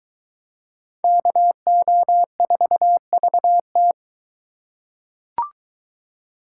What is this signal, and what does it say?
Morse code, heard as a keyed tone, about 22 wpm, 700 Hz. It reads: KO4VT